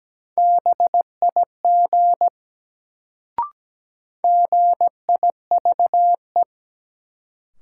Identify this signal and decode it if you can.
Morse code, heard as a keyed tone, about 17 wpm, 700 Hz. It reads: BIG GIVE
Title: Sentences from Top 200 Words - New Word Drill - 17wpm